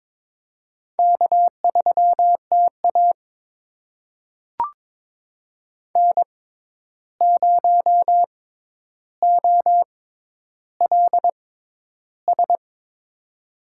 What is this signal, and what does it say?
Morse code, heard as a keyed tone, about 22 wpm, spaced as for 11 wpm, 700 Hz. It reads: K3TA N0OLS
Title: USA Call Signs - 6x Character Spacing - 22wpm